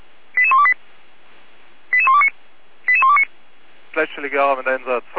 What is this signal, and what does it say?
Five-tone selective call (ZVEI), heard as quick runs of five tones, tones of about 70 ms, with speech. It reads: 88128 88129 88129